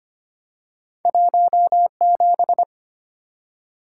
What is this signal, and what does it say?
Morse code, heard as a keyed tone, about 25 wpm, 700 Hz. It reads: 17
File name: Two Letter-Number Combinations - 25wpm